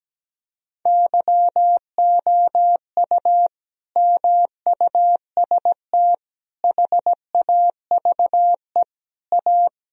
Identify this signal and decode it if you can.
Morse code, heard as a keyed tone, about 17 wpm, 700 Hz. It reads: YOU MUST HAVE A
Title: Sentences from Top 1000 Words - Review Part 4 - 17wpm